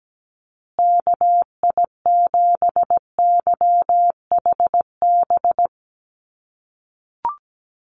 Morse code, heard as a keyed tone, about 17 wpm, 700 Hz. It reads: KI7YHB
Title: USA Call Signs Repeated 3x 17wpm